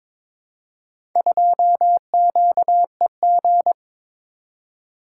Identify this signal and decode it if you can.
Morse code, heard as a keyed tone, about 22 wpm, 700 Hz. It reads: 2QEG
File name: practice / 22wpm / Four Letter-Number Combinations - 22wpm